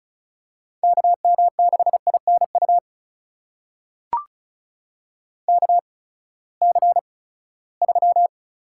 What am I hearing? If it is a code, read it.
Morse code, heard as a keyed tone, about 36 wpm, spaced as for 20 wpm, 700 Hz. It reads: KM6INU KC3
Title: USA Call Signs - 6 Characters - 8x Character Spacing - 35wpm